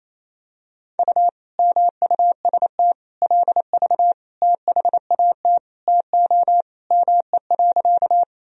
Morse code, heard as a keyed tone, about 28 wpm, 700 Hz. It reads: U MUST LV THAT TO ME.
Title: Sentences from Top 400 Words and Abbreviations 28wpm